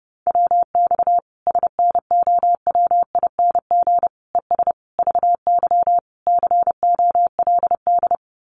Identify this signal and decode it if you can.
Morse code, heard as a keyed tone, about 30 wpm, 700 Hz. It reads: WX SNOWING ES VY COLD
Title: Common QSO Phrases 30wpm